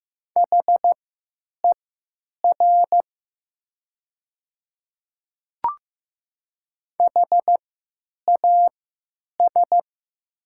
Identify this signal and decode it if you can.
Morse code, heard as a keyed tone, about 15 wpm, spaced as for 9 wpm, 700 Hz. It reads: HER HAS
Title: 3x Character Spacing - Top 100 Words - 15wpm